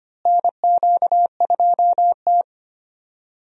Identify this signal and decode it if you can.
Morse code, heard as a keyed tone, about 25 wpm, 700 Hz. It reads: NQ2T
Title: USA Call Signs - 4 Characters - Repeated 3x - 25wpm